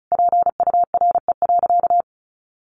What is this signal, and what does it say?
Morse code, heard as a keyed tone, about 34 wpm, 700 Hz. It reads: PURE.
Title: Sentences from Top 1500 Words 35wpm